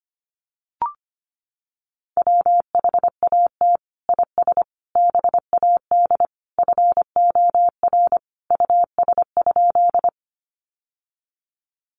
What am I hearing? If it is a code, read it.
Morse code, heard as a keyed tone, about 25 wpm, 700 Hz. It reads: WHAT IS BAD FOR US?
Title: Sentences from Top 700 Words 25wpm